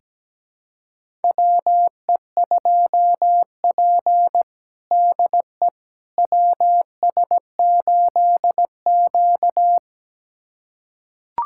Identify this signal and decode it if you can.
Morse code, heard as a keyed tone, about 17 wpm, 700 Hz. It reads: WE2P DE WS8Q
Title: USA Callsign DE Callsign - 4 Characters - 17wpm